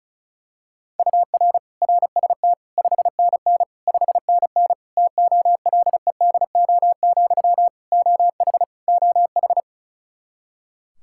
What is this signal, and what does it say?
Morse code, heard as a keyed tone, about 36 wpm, 700 Hz. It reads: UR RST 5NN 5NN TOLEDO, OH OH
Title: Practice QSO - Exchange 1 - Beginning - 35wpm